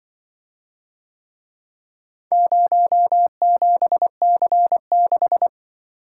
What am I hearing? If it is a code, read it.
Morse code, heard as a keyed tone, about 25 wpm, 700 Hz. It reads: 07C6
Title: Speed-Racing - Four Letter-Number Combinations - 20wpm